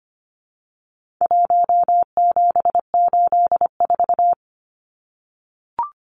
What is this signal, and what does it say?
Morse code, heard as a keyed tone, about 25 wpm, 700 Hz. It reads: 1784
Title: Numbers - 4-Digit 25wpm